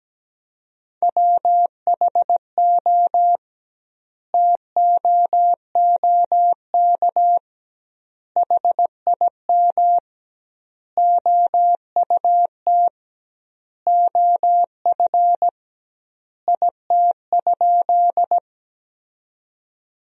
Morse code, heard as a keyed tone, about 17 wpm, 700 Hz. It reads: WHO TOOK HIM OUT OF IT?
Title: 2x Word-Spacing - Sentences from Top 300 Words - 17wpm